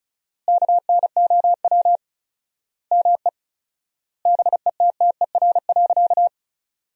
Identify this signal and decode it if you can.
Morse code, heard as a keyed tone, about 36 wpm, 700 Hz. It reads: KNOW ME BETTER.
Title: 4x Word-Spacing - Sentences from Top 400 Words - 35wpm